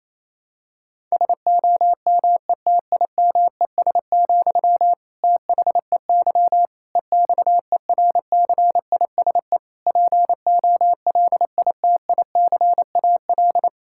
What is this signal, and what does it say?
Morse code, heard as a keyed tone, about 28 wpm, 700 Hz. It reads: SOMETIMES, THEY EXERCISE POLITICAL
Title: Sentences from Top 1000 Words - Review Part 6 - 28wpm